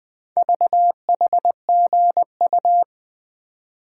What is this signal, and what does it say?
Morse code, heard as a keyed tone, about 20 wpm, 700 Hz. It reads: VHGU
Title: Four Letter-Number Combinations - 20wpm